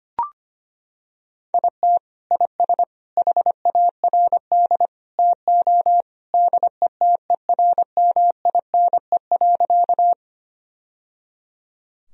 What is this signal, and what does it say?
Morse code, heard as a keyed tone, about 25 wpm, 700 Hz. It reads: IT IS HARD TO DETERMINE.
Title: Sentences from Top 1000 Words - Review Part 4 - 25wpm